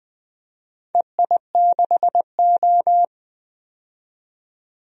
Morse code, heard as a keyed tone, about 20 wpm, 700 Hz. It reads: EI6O
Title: DXCC-100 Call Signs - 4 Characters - 20wpm